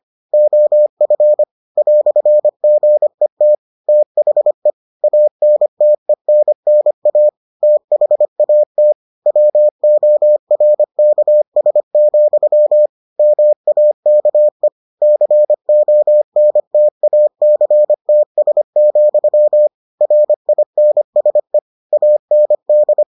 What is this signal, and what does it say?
Morse code, heard as a keyed tone, about 25 wpm, 600 Hz. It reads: OF "GET THE ANTENNA THAT WORKS, MAKE CONTACTS, RINSE AND